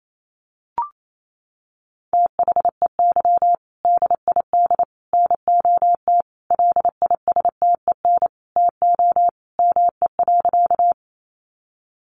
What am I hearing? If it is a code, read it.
Morse code, heard as a keyed tone, about 28 wpm, 700 Hz. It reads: THEY DID NOT LISTEN TO ME.
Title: Sentences from Top 600 Words 28wpm